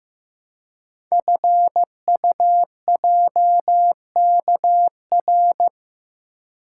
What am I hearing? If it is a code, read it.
Morse code, heard as a keyed tone, about 15 wpm, 700 Hz. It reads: FUJKR
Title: Code Groups - 5 Letters - 15wpm